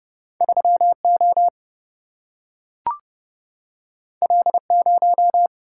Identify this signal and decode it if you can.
Morse code, heard as a keyed tone, about 29 wpm, 700 Hz. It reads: 3O L0
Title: Single-Letter Single-Number 30wpm